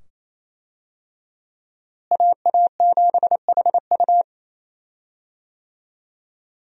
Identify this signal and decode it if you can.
Morse code, heard as a keyed tone, about 28 wpm, 700 Hz. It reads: AA7HU
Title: USA Call Signs - 5 Characters - Repeated 2x - 28wpm